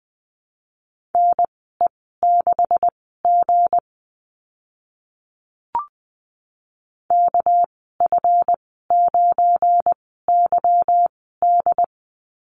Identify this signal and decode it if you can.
Morse code, heard as a keyed tone, about 20 wpm, spaced as for 14 wpm, 700 Hz. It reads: NE6G KF9YD
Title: 245 Lesson 39g - Letter z - Call Signs 20wpm